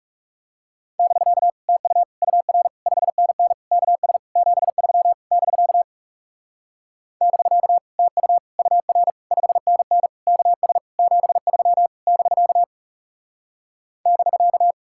Morse code, heard as a keyed tone, about 40 wpm, 700 Hz. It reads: <BK> TU UR 5NN KS 73 <BK> <BK> TU UR 5NN KS 73 <BK> <BK>